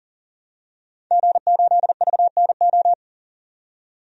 Morse code, heard as a keyed tone, about 40 wpm, 700 Hz. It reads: G8VNO
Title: DXCC-100 Call Signs - 5 Characters - 40wpm